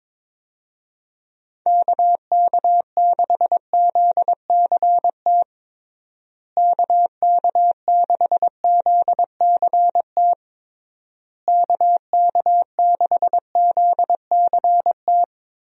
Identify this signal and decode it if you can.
Morse code, heard as a keyed tone, about 22 wpm, 700 Hz. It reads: KK6ZCT KK6ZCT KK6ZCT
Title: USA Call Signs - 6 Characters - Repeated 3x - 22wpm